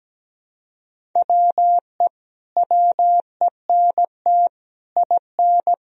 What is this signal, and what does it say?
Morse code, heard as a keyed tone, about 17 wpm, 700 Hz. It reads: WE WENT IN
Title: Sentences from Top 1000 Words - Review Part 2 - 17wpm